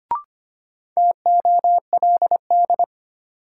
Morse code, heard as a keyed tone, about 25 wpm, 700 Hz. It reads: TOLD
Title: Sentences from Top 400 Words - New Word Drill - 25wpm